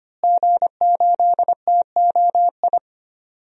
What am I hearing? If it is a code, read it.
Morse code, heard as a keyed tone, about 25 wpm, 700 Hz. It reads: G8TOI
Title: DXCC-100 Call Signs - 5 Characters - Repeated 3x - 25wpm